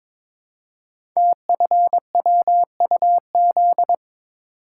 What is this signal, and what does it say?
Morse code, heard as a keyed tone, about 23 wpm, 700 Hz. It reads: TFWUZ
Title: Code Groups - 5 Letter:Numbers - 22wpm